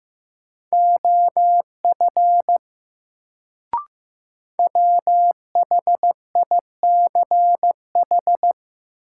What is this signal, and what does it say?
Morse code, heard as a keyed tone, about 15 wpm, 700 Hz. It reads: OF WHICH
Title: Sets of 2 Words - New Word Drill - 15wpm